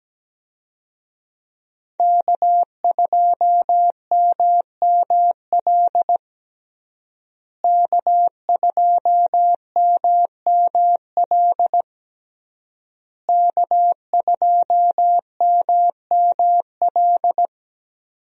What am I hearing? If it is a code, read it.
Morse code, heard as a keyed tone, about 17 wpm, 700 Hz. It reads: K2MML K2MML K2MML
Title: USA Call Signs - 5 Characters - Repeated 3x - 17wpm